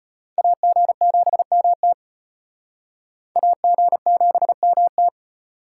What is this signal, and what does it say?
Morse code, heard as a keyed tone, about 34 wpm, 700 Hz. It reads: AG7MT AG7MT